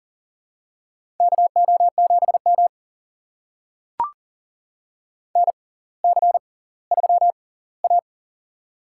Morse code, heard as a keyed tone, about 39 wpm, spaced as for 20 wpm, 700 Hz. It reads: KO7M NC3A
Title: USA Call Signs - 4 Characters - 6x Character Spacing - 40wpm